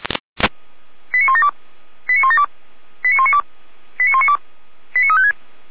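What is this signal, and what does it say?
Five-tone selective call (ZVEI), heard as quick runs of five tones, tones of about 70 ms. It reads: 89172 89172 89192 89192 89356
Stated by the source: ZVEI five-tone selective call received on an SDR.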